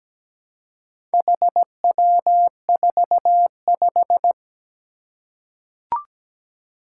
Morse code, heard as a keyed tone, about 17 wpm, 700 Hz. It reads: HW45
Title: Four Challenging Letter-Number Combinations 17wpm